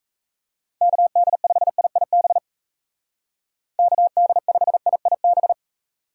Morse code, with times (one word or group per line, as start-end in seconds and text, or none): 0.8-2.4 KD5IIB
3.8-5.5 KD5IIB